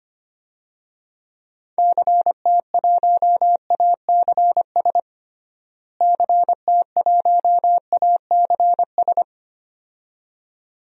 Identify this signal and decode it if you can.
Morse code, heard as a keyed tone, about 25 wpm, 700 Hz. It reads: CT1ACS CT1ACS
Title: DXCC-100 Call Signs - 6 Characters - Repeated 2x - 25wpm